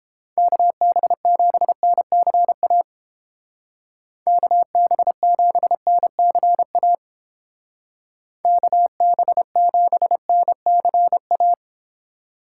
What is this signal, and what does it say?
Morse code, heard as a keyed tone, about 29 wpm, 700 Hz. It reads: KB7NCA KB7NCA KB7NCA